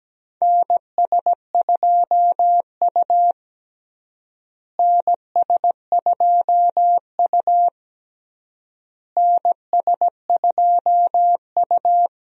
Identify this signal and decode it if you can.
Morse code, heard as a keyed tone, about 17 wpm, 700 Hz. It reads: NS2U NS2U NS2U
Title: USA Call Signs - 4 Characters - Repeated 3x - 17wpm